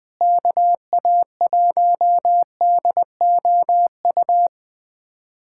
Morse code, heard as a keyed tone, about 20 wpm, 700 Hz. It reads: KA1DOU
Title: USA Callsign DE Callsign 20wpm